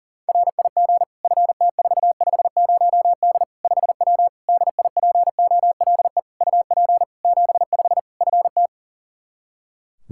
Morse code, heard as a keyed tone, about 40 wpm, 700 Hz. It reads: RIG FT450D 5W DIPOLE UP 75 FT
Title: Practice QSO - Exchange 2 - Info - 40wpm